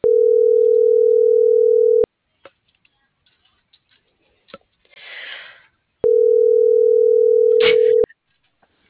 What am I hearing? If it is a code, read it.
Ringback tone (North America), 440 + 480 Hz.